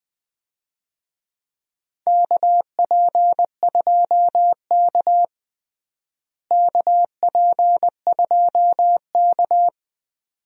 Morse code, heard as a keyed tone, about 20 wpm, 700 Hz. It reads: KP2K KP2K
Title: DXCC-100 Call Signs - 4 Characters - Repeated 2x - 20wpm